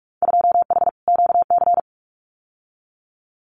Morse code, heard as a keyed tone, about 44 wpm, 700 Hz. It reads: 2H QC
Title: Winter Field Day Exchanges 45wpm